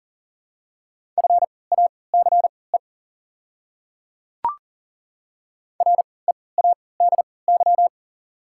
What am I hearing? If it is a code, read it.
Morse code, heard as a keyed tone, about 40 wpm, spaced as for 23 wpm, 700 Hz. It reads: FACE READY
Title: 3x Character Spacing - Top 500 Words - 40wpm